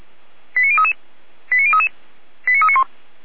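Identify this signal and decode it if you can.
Five-tone selective call (ZVEI), heard as quick runs of five tones, tones of about 80 ms. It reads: 89033 89033 89391
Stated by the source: ZVEI five-tone call 88341 received by SDR, logged by feuerwehr-orlamuende.org fire brigade server.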